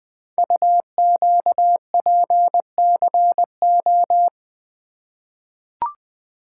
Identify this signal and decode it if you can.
Morse code, heard as a keyed tone, about 20 wpm, 700 Hz. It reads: UQPCO
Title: Code Groups - 5 Letters - 20wpm